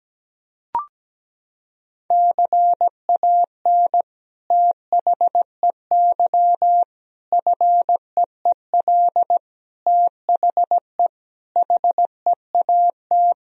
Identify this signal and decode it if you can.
Morse code, heard as a keyed tone, about 17 wpm, 700 Hz. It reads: CAN THEY FEEL THE HEAT
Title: Sentences from Top 1000 Words - Review Part 3 - 17wpm